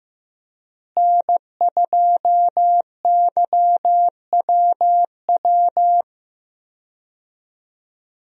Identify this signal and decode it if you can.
Morse code, heard as a keyed tone, about 15 wpm, 700 Hz. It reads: N2YWW
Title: USA Call Signs - 5 Characters - Repeated 2x - 15wpm